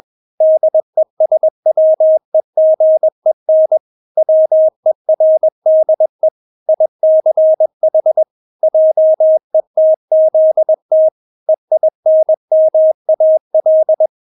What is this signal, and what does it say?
Morse code, heard as a keyed tone, about 21 wpm, 630 Hz. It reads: DESWEGEN WERDE ICH JETZT EINMAL
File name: Witz2 21wpm